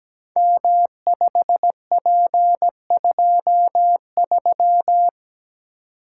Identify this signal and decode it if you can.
Morse code, heard as a keyed tone, about 17 wpm, 700 Hz. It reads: M5P23
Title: Code Groups - 5 Letter:Numbers - 17wpm